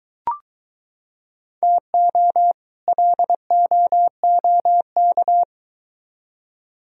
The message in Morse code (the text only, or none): TO LOOK